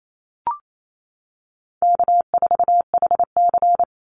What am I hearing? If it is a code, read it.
Morse code, heard as a keyed tone, about 28 wpm, 700 Hz. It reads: K4HC
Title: Easy Call Signs Repeated 2x - 28wpm